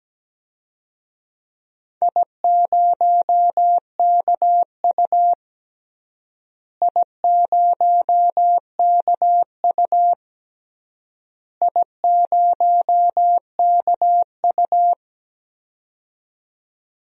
Morse code, heard as a keyed tone, about 17 wpm, 700 Hz. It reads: I0KU I0KU I0KU
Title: DXCC-100 Call Signs - 4 Characters - Repeated 3x - 17wpm